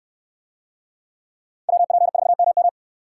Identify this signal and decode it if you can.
Morse code, heard as a keyed tone, about 69 wpm, 700 Hz. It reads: WP4MM